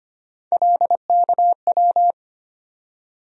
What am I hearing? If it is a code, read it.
Morse code, heard as a keyed tone, about 25 wpm, 700 Hz. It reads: LKW
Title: USA Call Sign Postfixes - 3 Letters 25wpm